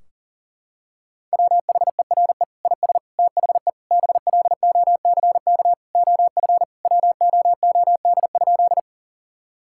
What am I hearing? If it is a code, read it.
Morse code, heard as a keyed tone, about 40 wpm, 700 Hz. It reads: WHERE IS THE BLOCK OF WOOD?
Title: Sentences from Top 1600 Words 40wpm